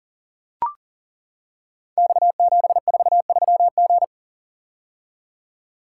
Morse code, heard as a keyed tone, about 39 wpm, 700 Hz. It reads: X743G
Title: Code Groups - 5 Letter:Numbers - 40wpm